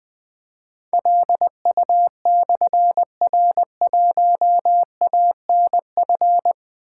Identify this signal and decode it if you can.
Morse code, heard as a keyed tone, about 20 wpm, 700 Hz. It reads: LU/R1ANF
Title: Difficult Call Signs 20wpm